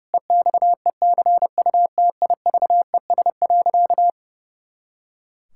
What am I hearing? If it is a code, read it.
Morse code, heard as a keyed tone, about 30 wpm, 700 Hz. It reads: EXECUTIVES.
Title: Pangram Challenge v2 - 30wpm